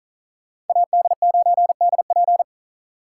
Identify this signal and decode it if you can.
Morse code, heard as a keyed tone, about 46 wpm, 700 Hz. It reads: AD9DP